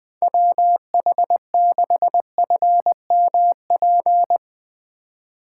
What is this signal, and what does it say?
Morse code, heard as a keyed tone, about 21 wpm, 700 Hz. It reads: WH6FMP
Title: Speed Racing - USA 2x3 Call Signs 20wpm